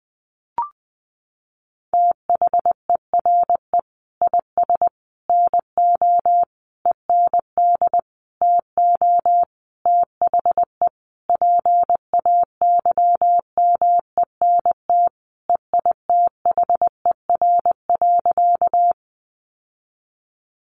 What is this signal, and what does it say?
Morse code, heard as a keyed tone, about 20 wpm, 700 Hz. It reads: THERE IS NO END TO THE PAYMENT EITHER.
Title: Sentences from Top 1200 Words 20wpm